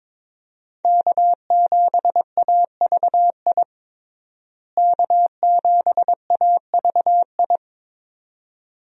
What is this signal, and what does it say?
Morse code, heard as a keyed tone, about 21 wpm, 700 Hz. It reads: K7AVI K7AVI
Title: USA Call Signs - 5 Characters - Repeated 2x - 22wpm